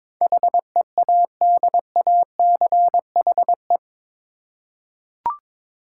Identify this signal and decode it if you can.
Morse code, heard as a keyed tone, about 22 wpm, 700 Hz. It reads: HEADACHE
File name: practice / 22wpm / Eight Letter Words - 22wpm